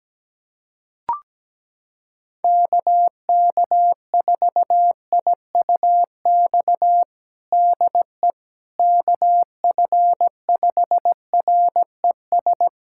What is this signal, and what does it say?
Morse code, heard as a keyed tone, about 17 wpm, 700 Hz. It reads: KK4IUX DE KF5RES
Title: USA Callsign DE Callsign 17wpm